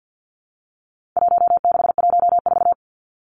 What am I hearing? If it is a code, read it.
Morse code, heard as a keyed tone, about 50 wpm, 700 Hz. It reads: 1614
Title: Numbers - Contest Style - 4-Digit 50wpm